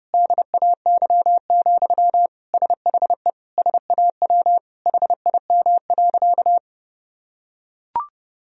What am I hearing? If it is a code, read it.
Morse code, heard as a keyed tone, about 31 wpm, 700 Hz. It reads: DAY, SHE SAW HIM.
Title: Sentences from Top 1000 Words - Review Part 4 - 30wpm